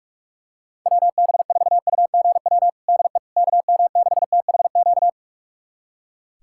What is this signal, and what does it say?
Morse code, heard as a keyed tone, about 46 wpm, 700 Hz. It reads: WB4UGW DE KM6THQ
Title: USA Callsign DE Callsign - 6 Characters - 45wpm